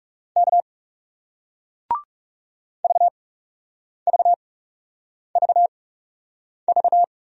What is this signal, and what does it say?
Morse code, heard as a keyed tone, about 40 wpm, spaced as for 8 wpm, 700 Hz. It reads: K VVVV